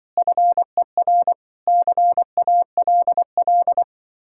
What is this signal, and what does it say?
Morse code, heard as a keyed tone, about 25 wpm, 700 Hz. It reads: FER CALL